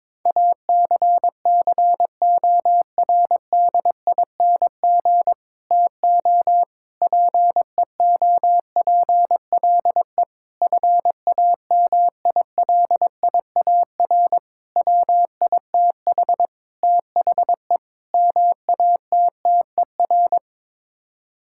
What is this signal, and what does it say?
Morse code, heard as a keyed tone, about 22 wpm, 700 Hz. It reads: ACCORDING TO PEOPLE FAMILIAR WITH THE MATTER
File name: practice / 22wpm / Sets of 7 Words - 22wpm